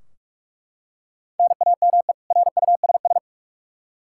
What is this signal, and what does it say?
Morse code, heard as a keyed tone, about 41 wpm, 700 Hz. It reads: NAME RUSS
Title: K1USN SST QSO Practice 45wpm